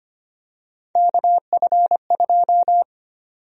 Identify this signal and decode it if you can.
Morse code, heard as a keyed tone, about 25 wpm, 700 Hz. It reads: KF2
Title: USA Call Sign Prefixes - Repeated 2x - 25wpm